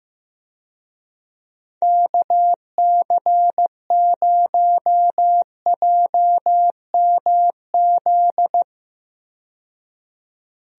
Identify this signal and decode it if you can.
Morse code, heard as a keyed tone, about 15 wpm, 700 Hz. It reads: KC0JMZ
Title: USA Call Signs - 6 Characters - 15wpm